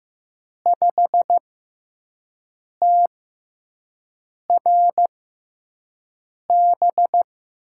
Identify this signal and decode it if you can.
Morse code, heard as a keyed tone, about 15 wpm, spaced as for 5 wpm, 700 Hz. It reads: 5TRB
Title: USA Call Signs - 6 Characters - 6x Character Spacing - 15wpm